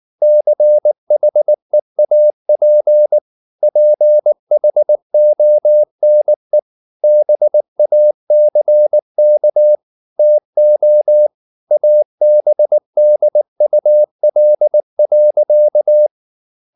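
Morse code, heard as a keyed tone, about 19 wpm, 600 Hz. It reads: CHEAP PHONE BACK TO ABDUL.
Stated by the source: Morse code at 19 wpm from CW med Gnister, reading Ken Follett's novel Never in English.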